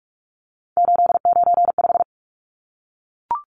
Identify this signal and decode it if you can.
Morse code, heard as a keyed tone, about 44 wpm, 700 Hz. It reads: 895
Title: Numbers - 3-Digit 45wpm